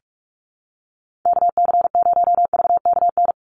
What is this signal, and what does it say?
Morse code, heard as a keyed tone, about 44 wpm, 700 Hz. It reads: KC0VKN
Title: Call Signs - 45wpm